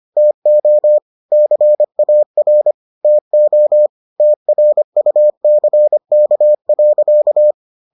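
Morse code, heard as a keyed tone, about 25 wpm, 600 Hz. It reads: TO CAR TO TRUCK.
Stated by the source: Morse edition of the Foundations of Amateur Radio episodes.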